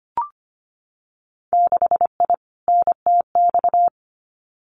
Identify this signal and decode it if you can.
Morse code, heard as a keyed tone, about 25 wpm, 690 Hz. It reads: 6I NTX
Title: Winter Field Day Exchanges Encore 25wpm